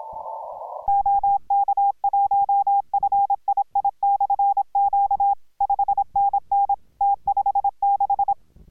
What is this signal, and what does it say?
Morse code, heard as a keyed tone, about 26 wpm, 810 Hz. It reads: OK1FII/Q 5NN T56